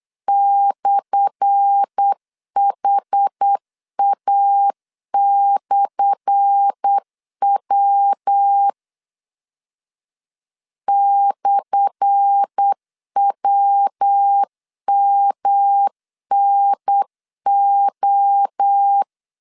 Morse code, heard as a keyed tone, about 8 wpm, 800 Hz. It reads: /HA/W /WMNO
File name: LEZIONI / 10 SIGNS / 10A SLASH 8wpm 3452